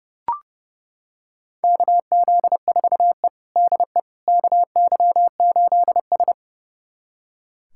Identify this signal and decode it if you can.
Morse code, heard as a keyed tone, about 30 wpm, 700 Hz. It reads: KZ4E DE KY8S
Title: USA Callsign DE Callsign - 4 Characters - 30wpm